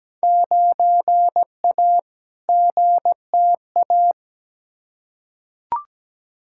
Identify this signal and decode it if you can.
Morse code, heard as a keyed tone, about 17 wpm, 700 Hz. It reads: 9A GTA
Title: Field Day Exchanges Encore - 17wpm